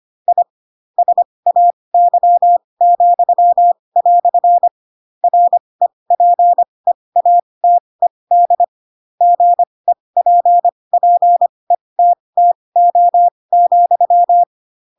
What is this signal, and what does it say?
Morse code, heard as a keyed tone, about 25 wpm, spaced as for 20 wpm, 700 Hz. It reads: I SAY," REPEATED GEPPETTO,
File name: Pinocchio / Pinocchio ch7 15 25 ik1hge